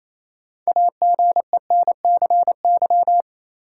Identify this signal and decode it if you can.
Morse code, heard as a keyed tone, about 27 wpm, 700 Hz. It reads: AGENCY
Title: Sentences from Top 1300 Words - New Word Drill - 28wpm